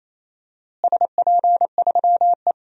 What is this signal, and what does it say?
Morse code, heard as a keyed tone, about 28 wpm, 700 Hz. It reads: SP3E
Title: DXCC-100 Call Signs - 4 Characters - Repeated 3x - 28wpm